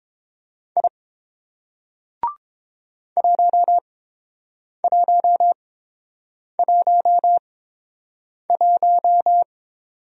Morse code, heard as a keyed tone, about 30 wpm, spaced as for 7 wpm, 700 Hz. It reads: I 1111